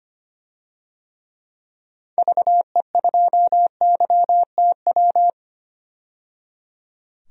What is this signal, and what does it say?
Morse code, heard as a keyed tone, about 25 wpm, 700 Hz. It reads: VE2YTW
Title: DXCC-100 Call Signs - 6 Characters - 25wpm